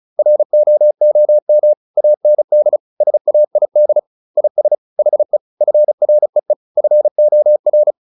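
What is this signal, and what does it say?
Morse code, heard as a keyed tone, about 34 wpm, 600 Hz. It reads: ROOM AND SAID IS HE FREE FOR
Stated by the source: Morse practice at 35 wpm from CW med Gnister, sending Ken Follett's novel Never in English.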